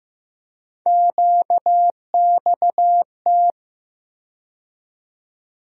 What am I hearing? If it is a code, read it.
Morse code, heard as a keyed tone, about 15 wpm, 700 Hz. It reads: QXT
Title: USA Call Sign Postfixes - 3 Letters 15wpm